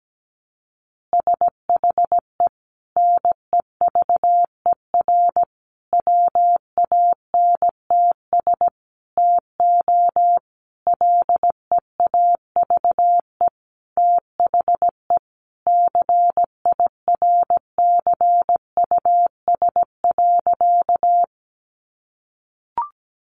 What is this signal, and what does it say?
Morse code, heard as a keyed tone, about 17 wpm, 700 Hz. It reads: SHE NEVER WANTS TO LEAVE THE CIRCUS.
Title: Easy Phrases 17wpm